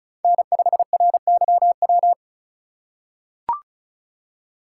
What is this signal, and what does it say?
Morse code, heard as a keyed tone, about 36 wpm, 700 Hz. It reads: N5RYW
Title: USA Call Signs - 5 Characters - 4x Character Spacing - 35wpm